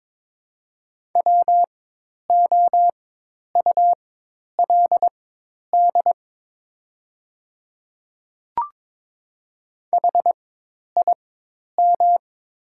Morse code, heard as a keyed tone, about 22 wpm, spaced as for 10 wpm, 690 Hz. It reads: WOULD HIM
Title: 4x Character Spacing - Top 200 Words - 22wpm